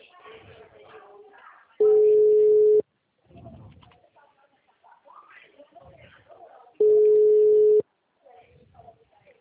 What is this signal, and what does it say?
Ringback tone (Europe, 425 Hz), with speech.
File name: out- -2020- - - .74439